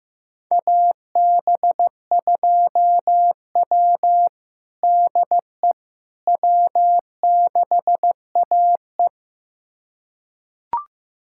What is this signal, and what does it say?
Morse code, heard as a keyed tone, about 16 wpm, 700 Hz. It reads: AB2W DE W6AE
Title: USA Callsign DE Callsign - 4 Characters - 15wpm